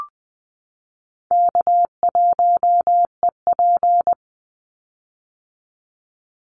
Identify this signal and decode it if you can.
Morse code, heard as a keyed tone, about 20 wpm, 700 Hz. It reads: K1EP
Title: Call Signs - 20wpm